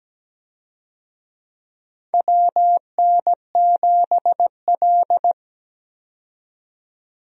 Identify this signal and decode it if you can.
Morse code, heard as a keyed tone, about 17 wpm, 700 Hz. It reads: WN7L
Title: USA Call Signs - 4 Characters - 17wpm